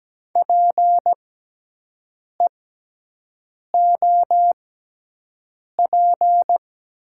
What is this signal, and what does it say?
Morse code, heard as a keyed tone, about 17 wpm, spaced as for 6 wpm, 700 Hz. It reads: PEOP
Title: 6x Character Spacing - Top 100 Words - 17wpm